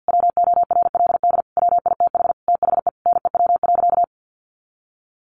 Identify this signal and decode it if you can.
Morse code, heard as a keyed tone, about 50 wpm, 700 Hz. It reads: WORLD WITH THE NEW.
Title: Sentences from Top 1300 Words 50wpm